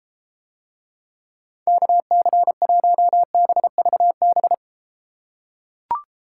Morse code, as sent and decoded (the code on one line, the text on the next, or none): -.- -.-. .---- -... ...- -...
KC1BVB